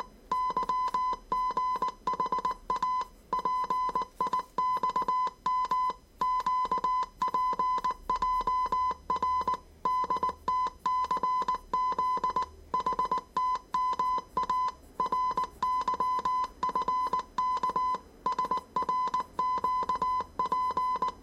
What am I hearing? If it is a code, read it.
Morse code, heard as a keyed tone, about 19 wpm, 1010 Hz. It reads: YGHA PIXM QPJR DTCZ HTMA RYFK SRQP
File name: 业余无线电 / ●未分类 / 字码80